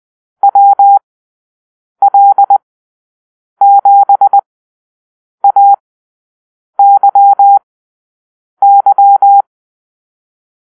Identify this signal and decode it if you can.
Morse code, heard as a keyed tone, about 20 wpm, spaced as for 7 wpm, 800 Hz. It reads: WL7AYY
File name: arrl / Call-Sign Drill-7